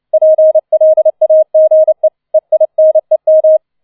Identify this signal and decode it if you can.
Morse code, heard as a keyed tone, about 29 wpm, 630 Hz. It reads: PLAGE EINEM
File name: 004 Vierter Streich 30wpm